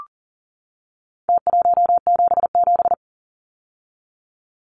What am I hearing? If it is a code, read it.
Morse code, heard as a keyed tone, about 37 wpm, 700 Hz. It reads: T177
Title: Numbers - Contest Style - 4-Digit 40wpm